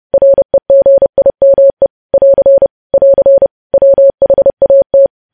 Morse code, heard as a keyed tone, about 30 wpm, 550 Hz. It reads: REGIME <AR> <AR> WHAT
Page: news headlines in morse code at 30 wpm